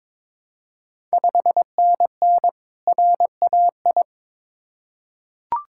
Morse code, heard as a keyed tone, about 22 wpm, 690 Hz. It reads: 5NN RAI
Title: Ontario QSO Party Exchange 22wpm